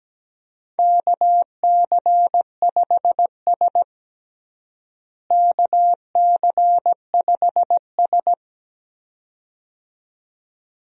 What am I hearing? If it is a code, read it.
Morse code, heard as a keyed tone, about 17 wpm, 700 Hz. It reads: KC5S KC5S
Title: USA Call Signs - 4 Characters - Repeated 2x - 17wpm